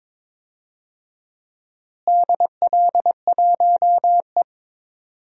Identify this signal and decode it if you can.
Morse code, heard as a keyed tone, about 22 wpm, 700 Hz. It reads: DL1E
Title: DXCC-100 Call Signs - 4 Characters - Repeated 2x - 22wpm